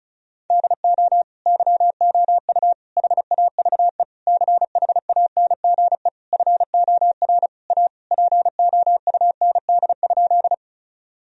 Morse code, heard as a keyed tone, about 36 wpm, 700 Hz. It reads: DO YOU HAVE CHANGE FOR A POUND?
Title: Sentences from Top 1000 Words - Review Part 2 - 35wpm